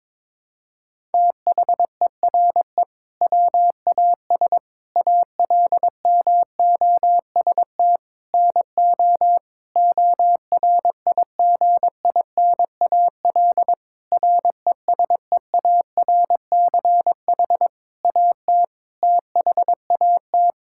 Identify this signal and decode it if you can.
Morse code, heard as a keyed tone, about 22 wpm, 700 Hz. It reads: THERE WAS ALMOST NO ORIGINAL RESEARCH AT THAT
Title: Sentences from Top 1000 Words - Review Part 6 - 22wpm